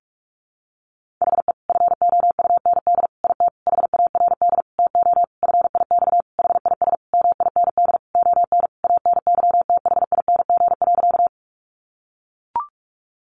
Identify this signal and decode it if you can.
Morse code, heard as a keyed tone, about 44 wpm, 700 Hz. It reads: HE FOUND IT HARD TO FIX HIS MIND ON ANYTHING.
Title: Sentences from Top 1400 Words 45wpm